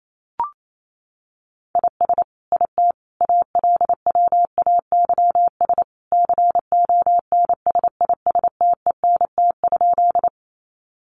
Morse code, heard as a keyed tone, about 28 wpm, 700 Hz. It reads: IS IT ALWAYS CONSISTENT?
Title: Sentences from Top 1500 Words 28wpm